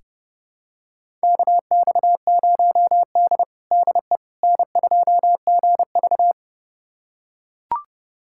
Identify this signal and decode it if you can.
Morse code, heard as a keyed tone, about 31 wpm, 700 Hz. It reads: KX0D DE N2GV
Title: USA Callsign DE Callsign - 4 Characters - 30wpm